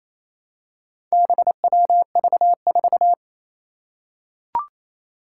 Morse code, heard as a keyed tone, about 27 wpm, 700 Hz. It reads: BWV4
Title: Four Challenging Letter-Number Combinations 28wpm